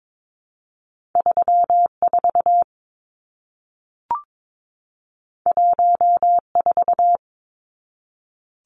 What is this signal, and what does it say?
Morse code, heard as a keyed tone, about 22 wpm, 700 Hz. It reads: 34 14
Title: Numbers - 2-Digit 22wpm